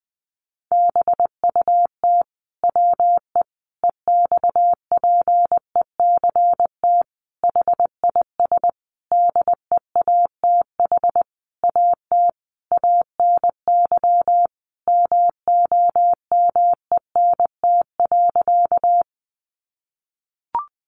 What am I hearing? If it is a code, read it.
Morse code, heard as a keyed tone, about 19 wpm, 700 Hz. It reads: BUT WE EXPECT HIS DEATH AT ANY MOMENT.
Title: Sentences from Top 1100 Words 20wpm